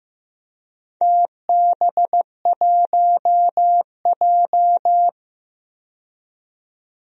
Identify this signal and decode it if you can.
Morse code, heard as a keyed tone, about 15 wpm, 700 Hz. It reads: TB1J
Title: DXCC-100 Call Signs - 4 Characters - 15wpm